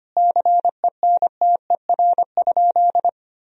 Morse code, heard as a keyed tone, about 25 wpm, 700 Hz. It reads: CENTER?
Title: Sentences from Top 1000 Words - Review Part 2 - 25wpm